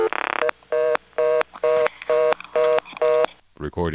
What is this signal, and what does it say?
Reorder tone (North America), 480 + 620 Hz, with speech.
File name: phonetrips / Kenly-ClaytonNC2HQ